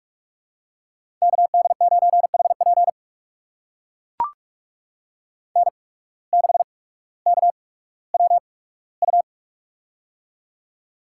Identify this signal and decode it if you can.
Morse code, heard as a keyed tone, about 44 wpm, spaced as for 19 wpm, 700 Hz. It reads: KD9HP N6KWU
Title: USA Call Signs - 5 Characters - 8x Character Spacing - 45wpm